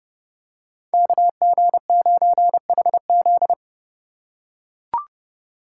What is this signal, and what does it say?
Morse code, heard as a keyed tone, about 32 wpm, 700 Hz. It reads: KG9HZ
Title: USA Call Signs - 5 Characters - 4x Character Spacing - 30wpm